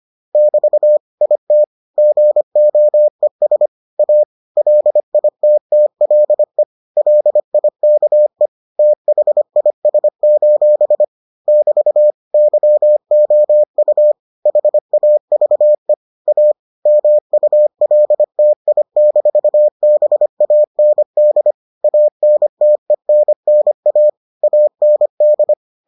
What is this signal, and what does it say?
Morse code, heard as a keyed tone, about 25 wpm, 600 Hz. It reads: <BT> IT GOES A LITTLE LIKE THIS: <BT> YOU HAVE A MULTI-BAND ANTENNA AND